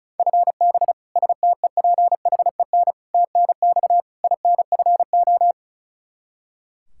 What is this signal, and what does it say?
Morse code, heard as a keyed tone, about 34 wpm, 700 Hz. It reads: FB STEPHEN TNX INFO
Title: Practice QSO - Exchange 3 - More Info - 35wpm